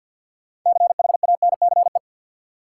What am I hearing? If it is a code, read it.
Morse code, heard as a keyed tone, about 53 wpm, 700 Hz. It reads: CHANCE
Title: Sentences from Top 800 Words - New Word Drill - 50wpm